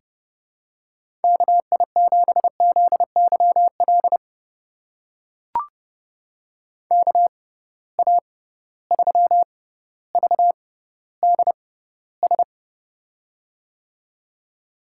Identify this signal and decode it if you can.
Morse code, heard as a keyed tone, about 30 wpm, spaced as for 15 wpm, 700 Hz. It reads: KI7ZYL KA3VDS
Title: USA Call Signs - 6 Characters - 6x Character Spacing - 30wpm